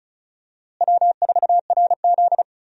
Morse code, heard as a keyed tone, about 36 wpm, 700 Hz. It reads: W4RZ
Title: USA Call Signs - 4 Characters - Repeated 3x - 35wpm